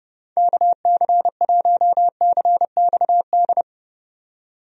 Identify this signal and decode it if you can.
Morse code, heard as a keyed tone, about 30 wpm, 700 Hz. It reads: KC1CXD
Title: USA Call Signs - 6 Characters - 30wpm